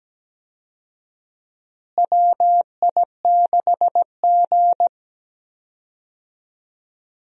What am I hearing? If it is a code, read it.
Morse code, heard as a keyed tone, about 17 wpm, 700 Hz. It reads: WI6G
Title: USA Call Signs - 4 Characters - Repeated 3x - 17wpm